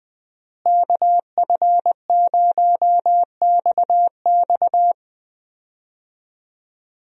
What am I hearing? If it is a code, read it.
Morse code, heard as a keyed tone, about 20 wpm, 700 Hz. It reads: KF0XX
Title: USA Call Signs - 5 Characters - Repeated 2x - 20wpm